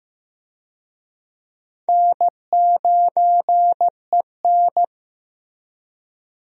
Morse code, heard as a keyed tone, about 15 wpm, 700 Hz. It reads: N9EN
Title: USA Call Signs - 4 Characters - Repeated 3x - 15wpm